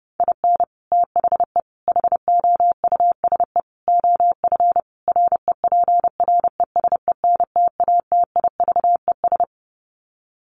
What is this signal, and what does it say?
Morse code, heard as a keyed tone, about 30 wpm, 700 Hz. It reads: IN THE HOUSE OF REPRESENTATIVES
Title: Sets of 5 Words Encore 30wpm